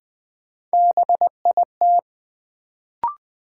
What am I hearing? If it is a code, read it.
Morse code, heard as a keyed tone, about 20 wpm, 700 Hz. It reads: BIT
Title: Sets of 3 Words - New Word Drill - 20wpm